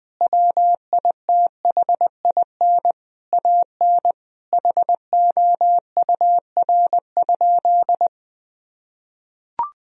Morse code, heard as a keyed tone, about 20 wpm, 700 Hz. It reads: WITHIN AN HOUR?
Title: Sentences from Top 1000 Words - Review Part 4 - 20wpm